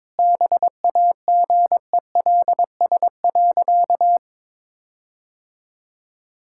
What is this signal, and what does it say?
Morse code, heard as a keyed tone, about 22 wpm, 700 Hz. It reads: BAGELS.
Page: morse-code-ninja-practice-22wpm